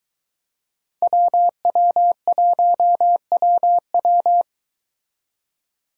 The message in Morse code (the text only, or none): WW1WW